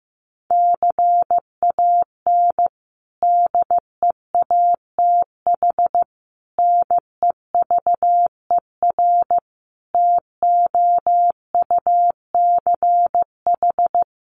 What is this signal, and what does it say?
Morse code, heard as a keyed tone, about 15 wpm, 700 Hz. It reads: CAN DEATH NEVER TOUCH
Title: Sentences from Top 1200 Words 15wpm